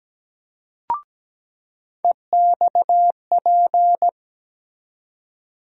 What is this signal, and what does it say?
Morse code, heard as a keyed tone, about 17 wpm, 700 Hz. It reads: EXP
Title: USA Call Sign Postfixes - 3 Letters 17wpm